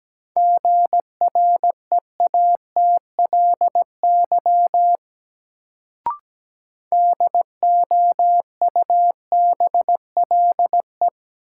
Morse code, heard as a keyed tone, about 17 wpm, 700 Hz. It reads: GREATLY DOUBLE
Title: Sentences from Top 1400 Words - New Word Drill Alt - 17wpm